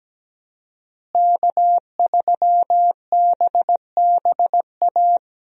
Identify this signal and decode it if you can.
Morse code, heard as a keyed tone, about 17 wpm, 700 Hz. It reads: K3BBA
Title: USA Call Signs - 5 Characters - Repeated 3x - 17wpm